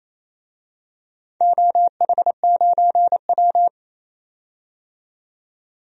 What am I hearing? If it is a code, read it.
Morse code, heard as a keyed tone, about 28 wpm, 700 Hz. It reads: OH9W
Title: DXCC-100 Call Signs - 4 Characters - 28wpm